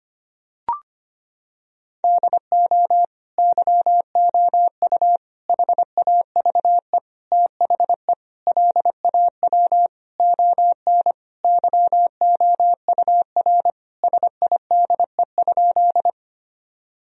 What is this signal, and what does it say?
Morse code, heard as a keyed tone, about 25 wpm, 700 Hz. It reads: DO YOU HAVE THE LAW ON YOUR SIDE?
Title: Sentences from Top 1000 Words - Review Part 3 - 25wpm